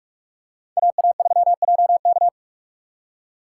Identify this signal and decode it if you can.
Morse code, heard as a keyed tone, about 44 wpm, 700 Hz. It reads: AA3JK